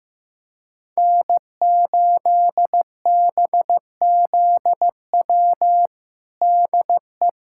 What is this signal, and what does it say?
Morse code, heard as a keyed tone, about 15 wpm, 700 Hz. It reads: N8BZW DE
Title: USA Callsign DE Callsign 15wpm